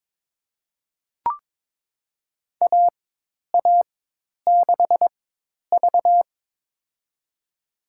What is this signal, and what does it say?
Morse code, heard as a keyed tone, about 21 wpm, spaced as for 10 wpm, 700 Hz. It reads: AA6V
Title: USA Call Signs - 4x Character Spacing - 22wpm